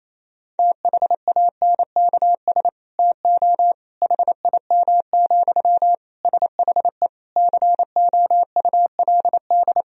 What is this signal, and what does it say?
Morse code, heard as a keyed tone, about 28 wpm, 700 Hz. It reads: THANKS TO HIM, SHE COULD
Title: Sentences from Top 1000 Words - Review Part 3 - 28wpm